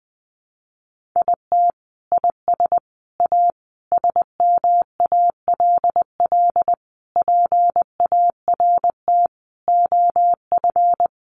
Morse code, heard as a keyed tone, about 20 wpm, 700 Hz. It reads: IT IS A SMALL PART OF
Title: Sentences from Top 1500 Words 20wpm